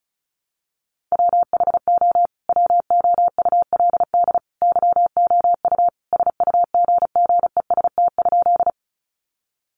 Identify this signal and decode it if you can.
Morse code, heard as a keyed tone, about 34 wpm, 700 Hz. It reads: WHO WOULD YOU SUGGEST?
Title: Sentences from Top 1300 Words 35wpm